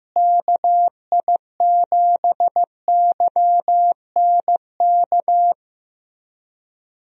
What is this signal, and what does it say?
Morse code, heard as a keyed tone, about 15 wpm, 700 Hz. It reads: KI7YNK
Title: USA Call Signs - 6 Characters - Repeated 3x - 15wpm